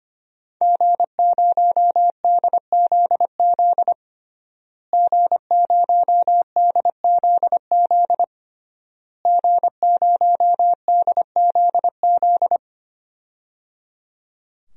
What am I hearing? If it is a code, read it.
Morse code, heard as a keyed tone, about 25 wpm, 700 Hz. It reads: G0DZZ G0DZZ G0DZZ
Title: DXCC-100 Call Signs - 5 Characters - Repeated 3x - 25wpm